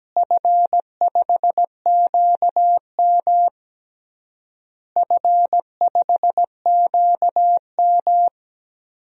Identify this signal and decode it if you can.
Morse code, heard as a keyed tone, about 17 wpm, 700 Hz. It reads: F5QM F5QM
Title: DXCC-100 Call Signs - 4 Characters - Repeated 3x - 17wpm